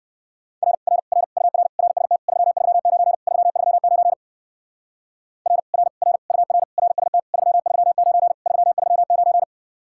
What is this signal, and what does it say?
Morse code, heard as a keyed tone, about 55 wpm, 700 Hz. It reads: R R R UR RST 339 339 R R R UR RST 339 339
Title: Speed-Racing Common QSO Phrases - 45wpm